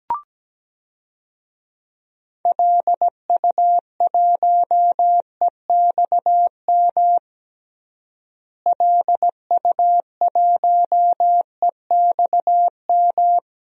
Morse code, heard as a keyed tone, about 17 wpm, 700 Hz. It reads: LU1EXM LU1EXM
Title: DXCC-100 Call Signs - 6 Characters - Repeated 3x - 17wpm